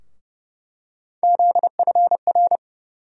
Morse code, heard as a keyed tone, about 30 wpm, 700 Hz. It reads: ZFR
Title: USA Call Sign Postfixes - 3 Letters 30wpm